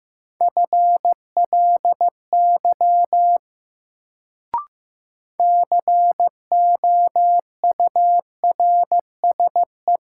Morse code, heard as a keyed tone, about 15 wpm, 700 Hz. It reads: FLY COURSE
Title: Sentences from Top 500 Words - New Word Drill - 15wpm